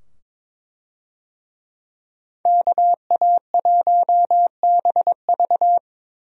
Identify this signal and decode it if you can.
Morse code, heard as a keyed tone, about 22 wpm, 700 Hz. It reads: KA1BV
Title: USA Call Signs - 5 Characters - 22wpm